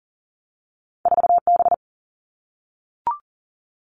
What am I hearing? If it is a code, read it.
Morse code, heard as a keyed tone, about 38 wpm, 700 Hz. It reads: 4 B
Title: Two Challenging Letter-Number Combinations 40wpm